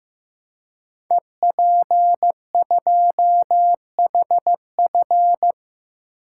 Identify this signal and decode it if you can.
Morse code, heard as a keyed tone, about 15 wpm, 700 Hz. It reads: EP2HF
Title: Code Groups - 5 Letter:Numbers - 15wpm